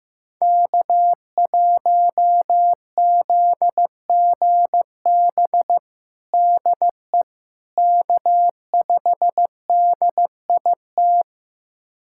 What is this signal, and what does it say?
Morse code, heard as a keyed tone, about 15 wpm, 700 Hz. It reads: K1ZGB DE K5DIT
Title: USA Callsign DE Callsign - 5 Characters - 15wpm